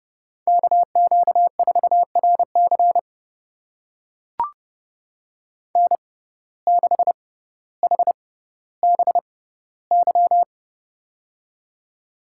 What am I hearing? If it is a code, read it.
Morse code, heard as a keyed tone, about 30 wpm, spaced as for 15 wpm, 700 Hz. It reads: KQ4RC N6HBY
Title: USA Call Signs - 5 Characters - 6x Character Spacing - 30wpm